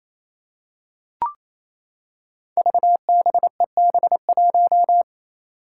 Morse code, heard as a keyed tone, about 28 wpm, 700 Hz. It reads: VBEB1
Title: Five Challenging Letter-Number Combinations 28wpm